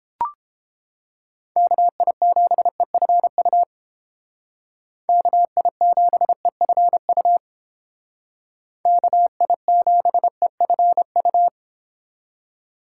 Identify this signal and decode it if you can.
Morse code, heard as a keyed tone, about 31 wpm, 700 Hz. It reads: KI7EFU KI7EFU KI7EFU